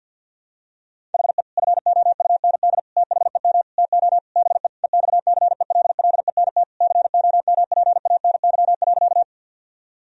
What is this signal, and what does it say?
Morse code, heard as a keyed tone, about 50 wpm, 700 Hz. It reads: HE FOUND THEM TO BE EXCELLENT COMPANY.
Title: Sentences from Top 1000 Words - Review Part 3 - 50wpm